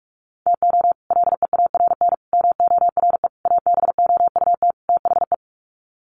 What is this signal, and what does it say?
Morse code, heard as a keyed tone, about 44 wpm, 700 Hz. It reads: TO LEARN MORE ABOUT THE
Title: Sets of 5 Words Encore2 45wpm